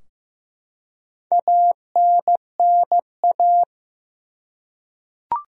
Morse code, heard as a keyed tone, about 16 wpm, 700 Hz. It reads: ANNA
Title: USA Common Names - Max 4 Letters - 15wpm